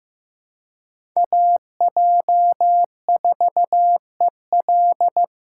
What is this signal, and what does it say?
Morse code, heard as a keyed tone, about 15 wpm, 700 Hz. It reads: AJ4EL
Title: USA Call Signs - 5 Characters - 4x Character Spacing - 15wpm